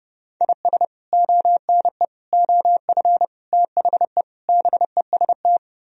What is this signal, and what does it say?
Morse code, heard as a keyed tone, about 33 wpm, 700 Hz. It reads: IS ONE OF THE BEST